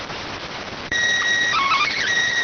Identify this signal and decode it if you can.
SSTV (slow-scan television), PD 290.